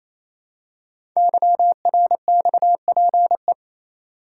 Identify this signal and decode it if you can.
Morse code, heard as a keyed tone, about 27 wpm, 700 Hz. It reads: YRXPE
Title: Code Groups - 5 Letters - 28wpm